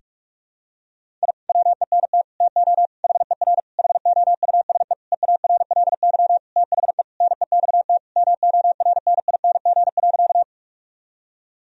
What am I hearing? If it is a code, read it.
Morse code, heard as a keyed tone, about 42 wpm, 700 Hz. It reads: I WENT TO HER HOUSE EARLY THE NEXT MORNING.
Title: Sentences from Top 1000 Words - Review Part 2 - 45wpm